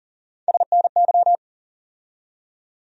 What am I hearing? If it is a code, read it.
Morse code, heard as a keyed tone, about 40 wpm, 700 Hz. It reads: SNY
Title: USA Call Sign Postfixes - 3 Letters 40wpm